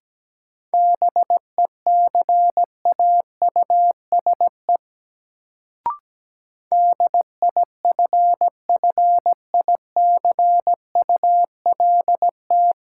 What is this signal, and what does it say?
Morse code, heard as a keyed tone, about 17 wpm, 700 Hz. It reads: BECAUSE DIFFICULT
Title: Sentences from Top 600 Words - New Word Drill - 17wpm